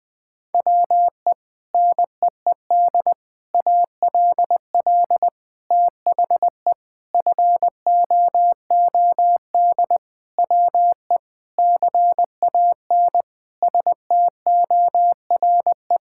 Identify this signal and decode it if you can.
Morse code, heard as a keyed tone, about 20 wpm, 700 Hz. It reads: WE NEED ALL THE FOOD WE CAN STORE
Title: Sentences from Top 1000 Words - Review Part 5 - 20wpm